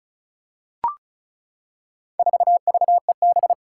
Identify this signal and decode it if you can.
Morse code, heard as a keyed tone, about 36 wpm, 700 Hz. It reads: 4VEB
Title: Four Challenging Letter-Number Combinations 35wpm